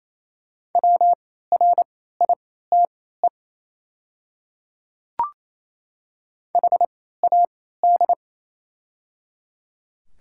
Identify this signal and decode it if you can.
Morse code, heard as a keyed tone, about 27 wpm, spaced as for 16 wpm, 690 Hz. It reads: WRITE HAD
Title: 3x Character Spacing - Top 200 Words - 28wpm